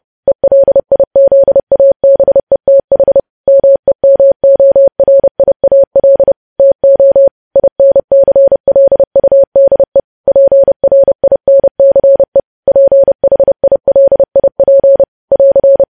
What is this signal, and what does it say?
Morse code, heard as a keyed tone, about 30 wpm, 550 Hz. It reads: ELIZABETH MEMORIAL TO INCLUDE PRINCE PHILIP <AR>
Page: news headlines in morse code at 30 wpm